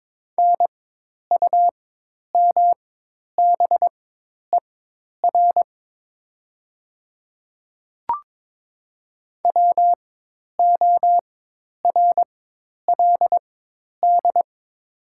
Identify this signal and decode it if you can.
Morse code, heard as a keyed tone, about 23 wpm, spaced as for 10 wpm, 700 Hz. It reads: NUMBER WORLD
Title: 4x Character Spacing - Top 200 Words - 22wpm